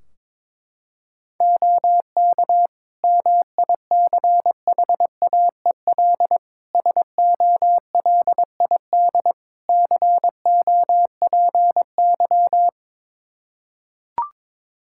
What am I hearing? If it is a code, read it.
Morse code, heard as a keyed tone, about 22 wpm, 700 Hz. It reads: OK MICHAEL SOLID COPY
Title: K1USN SST QSO Practice 22wpm